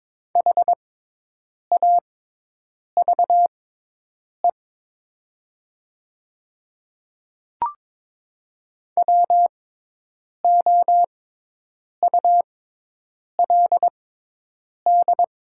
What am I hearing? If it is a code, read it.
Morse code, heard as a keyed tone, about 22 wpm, spaced as for 8 wpm, 700 Hz. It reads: HAVE WOULD
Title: 6x Character Spacing - Top 200 Words - 22wpm